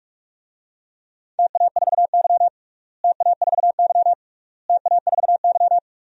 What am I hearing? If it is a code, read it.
Morse code, heard as a keyed tone, about 46 wpm, 700 Hz. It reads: TA4Y TA4Y TA4Y
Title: DXCC-100 Call Signs - 4 Characters - Repeated 3x - 45wpm